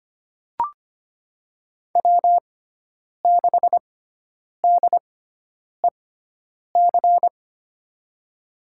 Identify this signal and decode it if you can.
Morse code, heard as a keyed tone, about 25 wpm, spaced as for 9 wpm, 700 Hz. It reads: W6DEC
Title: USA Call Signs - 6x Character Spacing - 25wpm